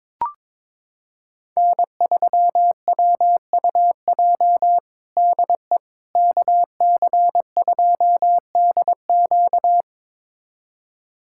Morse code, heard as a keyed tone, about 21 wpm, 700 Hz. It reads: N3WUJ DE KC2DQ
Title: USA Callsign DE Callsign - 5 Characters - 22wpm